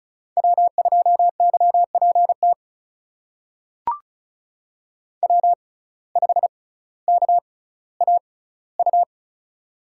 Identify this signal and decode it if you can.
Morse code, heard as a keyed tone, about 36 wpm, spaced as for 18 wpm, 700 Hz. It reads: W2YPT W5KAU
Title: USA Call Signs - 5 Characters - 6x Character Spacing - 35wpm